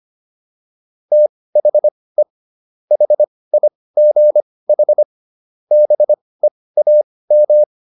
Morse code, heard as a keyed tone, about 25 wpm, spaced as for 18 wpm, 600 Hz. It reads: THE HIGH BEAM